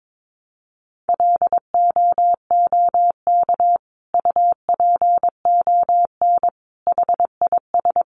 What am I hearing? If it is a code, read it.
Morse code, heard as a keyed tone, about 22 wpm, 700 Hz. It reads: LOOK UPON HIS
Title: Sentences from Top 1100 Words 22wpm